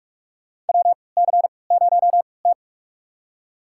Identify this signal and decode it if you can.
Morse code, heard as a keyed tone, about 41 wpm, spaced as for 24 wpm, 700 Hz. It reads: WC0T
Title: USA Call Signs - 4 Characters - 3x Character Spacing - 45wpm